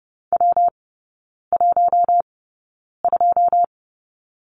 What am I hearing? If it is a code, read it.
Morse code, heard as a keyed tone, about 30 wpm, spaced as for 9 wpm, 700 Hz. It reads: W12
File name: Lesson 19e - Letter w - Three Characters 30wpm